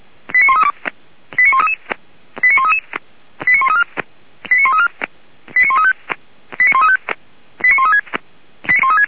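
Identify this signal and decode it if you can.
Five-tone selective call (ZVEI), heard as quick runs of five tones, tones of about 80 ms. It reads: 89132 89133 89133 89134 89134 89135 89135 89136
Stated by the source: Five-tone ZVEI selective call 89133, received via SDR by feuerwehr-orlamuende fire brigade.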